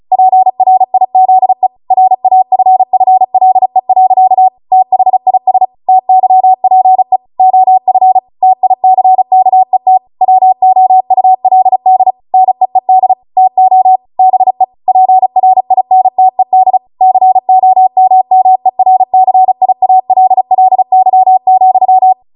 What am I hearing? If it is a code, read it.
Morse code, heard as a keyed tone, about 36 wpm, 750 Hz. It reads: PRIZE RAFFLE. THIS TYPE OF TICKET WOULD NEED TO BE PRINTED COMMERCIALLY,
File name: Morse / 35WPM